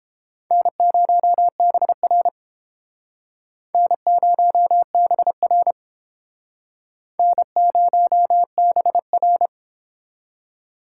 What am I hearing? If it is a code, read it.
Morse code, heard as a keyed tone, about 30 wpm, 700 Hz. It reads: N0BR N0BR N0BR